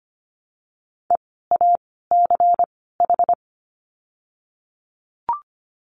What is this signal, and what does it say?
Morse code, heard as a keyed tone, about 24 wpm, spaced as for 16 wpm, 700 Hz. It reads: EACH
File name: course / 25:10wpm / 094 Lesson 16f - Letter c - Words 25wpm